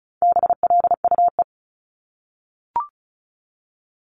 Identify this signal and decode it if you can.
Morse code, heard as a keyed tone, about 36 wpm, 700 Hz. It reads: BLUE
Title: Multi-letter Phonemes 35wpm